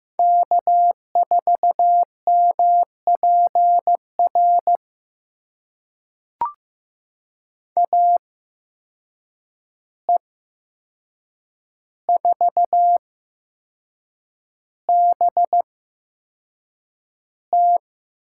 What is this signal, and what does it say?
Morse code, heard as a keyed tone, about 15 wpm, spaced as for 6 wpm, 700 Hz. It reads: K4MPR AE4BT
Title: USA Call Signs - 5 Characters - 8x Character Spacing - 15wpm